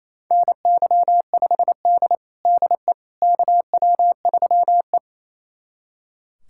Morse code, heard as a keyed tone, about 27 wpm, 690 Hz. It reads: NY5D DE KW3E
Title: USA Callsign DE Callsign - 4 Characters - 28wpm